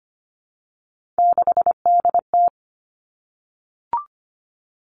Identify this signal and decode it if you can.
Morse code, heard as a keyed tone, about 25 wpm, 700 Hz. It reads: 6DT
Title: All Cut Numbers - 3-Digit 25wpm